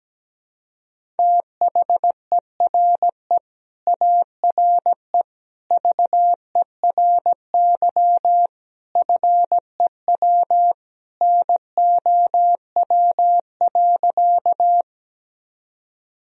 Morse code, heard as a keyed tone, about 17 wpm, 700 Hz. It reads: THERE ARE VERY FEW NOW.
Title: Sentences from Top 1000 Words - Review Part 1 - 17wpm